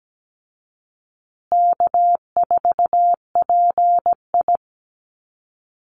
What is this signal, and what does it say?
Morse code, heard as a keyed tone, about 17 wpm, 700 Hz. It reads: K4PI
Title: Call Signs Repeated 3x - 17wpm